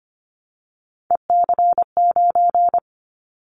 Morse code, heard as a keyed tone, about 25 wpm, 700 Hz. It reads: EC9
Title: Callsign Prefixes for Top 100 DXCC - 25wpm